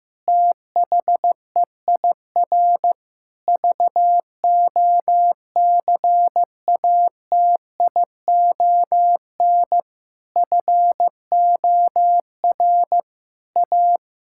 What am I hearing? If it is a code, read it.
Morse code, heard as a keyed tone, about 15 wpm, 700 Hz. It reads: THEIR VOCATION FOR A
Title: Pangram Challenge v2 - 15wpm